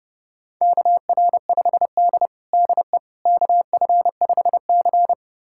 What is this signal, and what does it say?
Morse code, heard as a keyed tone, about 32 wpm, 700 Hz. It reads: KR5D DE KF5C
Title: USA Callsign DE Callsign - 4 Characters - 30wpm